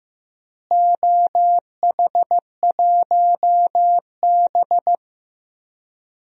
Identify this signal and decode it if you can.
Morse code, heard as a keyed tone, about 15 wpm, 700 Hz. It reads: OH1B
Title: DXCC-100 Call Signs - 4 Characters - Repeated 3x - 15wpm